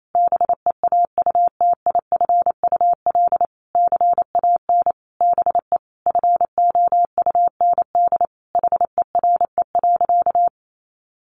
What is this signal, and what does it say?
Morse code, heard as a keyed tone, about 28 wpm, 700 Hz. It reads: BEAUTIFUL CAN BE FOUND HERE.
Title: Sentences from Top 1500 Words 28wpm